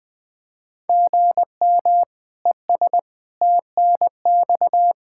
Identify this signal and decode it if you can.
Morse code, heard as a keyed tone, about 20 wpm, 700 Hz. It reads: GM ES TNX
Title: K1USN SST QSO Practice 20wpm